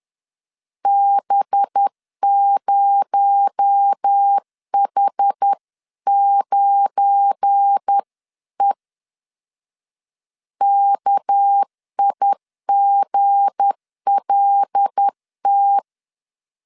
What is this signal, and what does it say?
Morse code, heard as a keyed tone, about 10 wpm, 800 Hz. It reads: B0H9E KIGLT
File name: LEZIONI / 8 YJ45 / 8 YJ 10wpm 2651